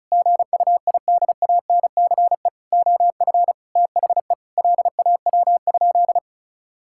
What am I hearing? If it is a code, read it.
Morse code, heard as a keyed tone, about 36 wpm, 700 Hz. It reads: GUIDANCE OF THE LAW?
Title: Sentences from Top 1600 Words 35wpm